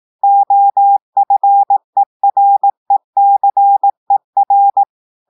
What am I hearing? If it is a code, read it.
Morse code, heard as a keyed tone, about 18 wpm, 800 Hz. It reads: OFERECER
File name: audios / portugais / portugais texte en clair 2 vitesse 18 mots minutes